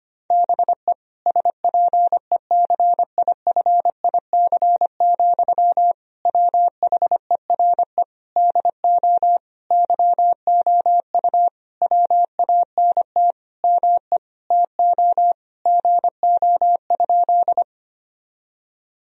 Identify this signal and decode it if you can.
Morse code, heard as a keyed tone, about 25 wpm, 700 Hz. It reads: BE SPECIFIC, WHERE DO YOU WANT ME TO GO?
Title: Sentences from Top 1000 Words - Review Part 3 - 25wpm